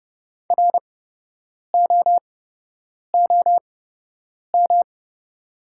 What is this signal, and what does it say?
Morse code, heard as a keyed tone, about 30 wpm, spaced as for 8 wpm, 690 Hz. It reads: ROOM
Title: 8x Character Spacing - Top 500 Words - 30wpm